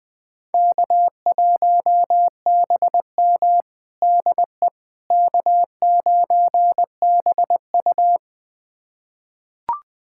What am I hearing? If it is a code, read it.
Morse code, heard as a keyed tone, about 21 wpm, 700 Hz. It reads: K1BM DE K9BU
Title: USA Callsign DE Callsign - 4 Characters - 20wpm